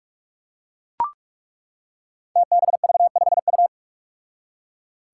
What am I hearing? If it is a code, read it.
Morse code, heard as a keyed tone, about 41 wpm, 700 Hz. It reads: TBV5U
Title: Five Challenging Letter-Number Combinations 45wpm